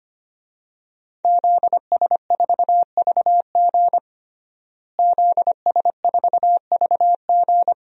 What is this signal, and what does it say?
Morse code, heard as a keyed tone, about 25 wpm, 700 Hz. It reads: ZS4VG ZS4VG
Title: DXCC-100 Call Signs - 5 Characters - Repeated 3x - 25wpm